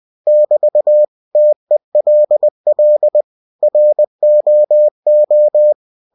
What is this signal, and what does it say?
Morse code, heard as a keyed tone, about 20 wpm, 600 Hz. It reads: <BT> TELL ROO
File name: cw-books / winnie-the-pooh / winnie-20wpm-page-0087